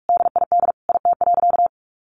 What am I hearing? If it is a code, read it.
Morse code, heard as a keyed tone, about 44 wpm, 700 Hz. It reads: DID IT.
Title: Sentences from Top 1500 Words 45wpm